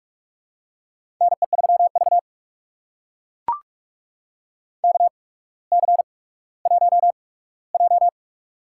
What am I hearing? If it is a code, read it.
Morse code, heard as a keyed tone, about 44 wpm, spaced as for 19 wpm, 700 Hz. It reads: NE3V KC1J
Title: USA Call Signs - 4 Characters - 8x Character Spacing - 45wpm